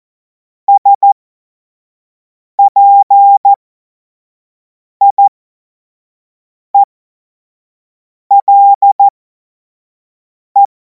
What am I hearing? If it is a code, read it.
Morse code, heard as a keyed tone, about 14 wpm, spaced as for 5 wpm, 800 Hz. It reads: SPIELE
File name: Morsekurs / Witze / Witz 05. Mai 2026 25 F70